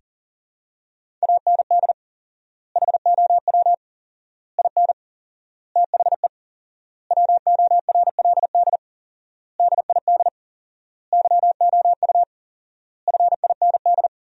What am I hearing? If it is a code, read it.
Morse code, heard as a keyed tone, about 40 wpm, 700 Hz. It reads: AND HOW IN THE WORLD DID YOU FIND
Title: 4x Word-Spacing - Sentences from Top 200 Words - 40wpm